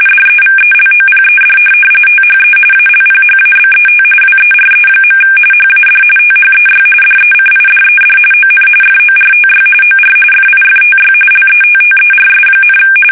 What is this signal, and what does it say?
RTTY, about 45.45 baud. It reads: DEVELOPED THE ALPHABET WERE. HAD THEY DECIDED ON FORTY OR FIFTY LETTERS,
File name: fsk850